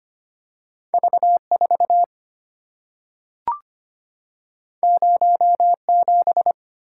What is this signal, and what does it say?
Morse code, heard as a keyed tone, about 25 wpm, 700 Hz. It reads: V4 07
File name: Two Letter-Number Combinations - 25wpm